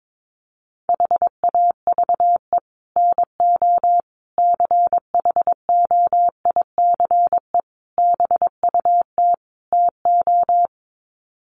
Morse code, heard as a keyed tone, about 22 wpm, 700 Hz. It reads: HAVE NO CHOICE BUT TO
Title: Sets of 5 Words Encore 22wpm